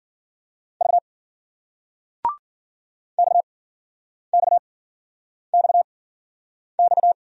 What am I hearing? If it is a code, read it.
Morse code, heard as a keyed tone, about 55 wpm, spaced as for 9 wpm, 700 Hz. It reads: V XXXX